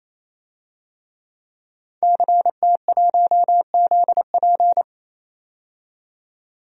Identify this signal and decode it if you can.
Morse code, heard as a keyed tone, about 28 wpm, 700 Hz. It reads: CT1ZP
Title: DXCC-100 Call Signs - 5 Characters - 28wpm